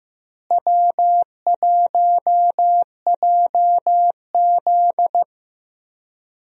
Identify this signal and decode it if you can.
Morse code, heard as a keyed tone, about 15 wpm, 700 Hz. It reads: W1JZ
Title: USA Call Signs - 4 Characters - Repeated 3x - 15wpm